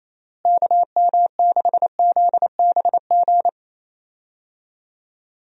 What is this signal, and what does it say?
Morse code, heard as a keyed tone, about 27 wpm, 700 Hz. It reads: KM6ZBG
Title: USA Call Signs - 6 Characters - Repeated 2x - 28wpm